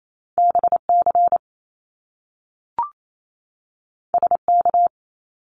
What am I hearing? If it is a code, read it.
Morse code, heard as a keyed tone, about 28 wpm, 700 Hz. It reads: BC SK
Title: Canadian Provinces and Territories 28wpm